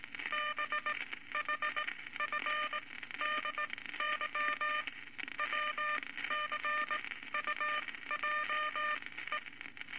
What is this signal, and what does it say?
Morse code, heard as a keyed tone, about 19 wpm, spaced as for 14 wpm, 1200 Hz, with noise. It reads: BHFDY WCUJE
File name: Morse Code Records / 19 A LCM 166 Ten Words Per Minute